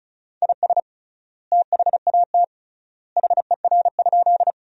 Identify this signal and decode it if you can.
Morse code, heard as a keyed tone, about 36 wpm, 700 Hz. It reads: IS THAT HER?
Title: 3x Word-Spacing - Sentences from Top 100 Words - 35wpm